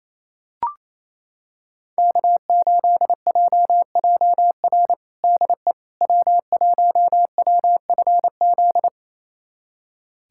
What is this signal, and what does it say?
Morse code, heard as a keyed tone, about 28 wpm, 700 Hz. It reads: K8JJR DE W1WFZ
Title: USA Callsign DE Callsign - 5 Characters - 28wpm